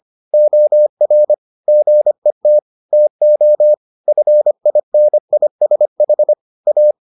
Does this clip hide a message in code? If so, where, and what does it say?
Yes: Morse code from 0.3 to 6.9 s, heard as a keyed tone, about 25 wpm, 600 Hz. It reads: OR GET TO FINISH A